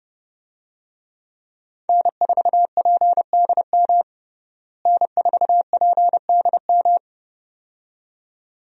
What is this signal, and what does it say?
Morse code, heard as a keyed tone, about 30 wpm, 700 Hz. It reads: N4PDM N4PDM
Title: USA Call Signs - 5 Characters - Repeated 2x - 30wpm